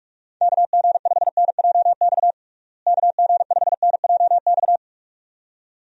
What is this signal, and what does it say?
Morse code, heard as a keyed tone, about 46 wpm, 700 Hz. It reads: KG5NJX KG5NJX
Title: USA Call Signs - 6 Characters - Repeated 2x - 45wpm